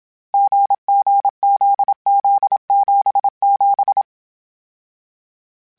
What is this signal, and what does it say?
Morse code, heard as a keyed tone, about 27 wpm, 800 Hz. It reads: GGZZ77